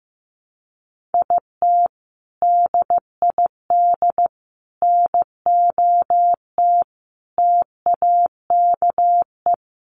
Morse code, heard as a keyed tone, about 15 wpm, 700 Hz. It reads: IT DID NOT TAKE
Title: Sets of 5 Words Encore2 15wpm